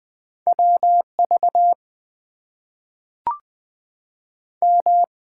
Morse code, heard as a keyed tone, about 20 wpm, 700 Hz. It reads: WV M
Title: US States and Canadian Provinces 20wpm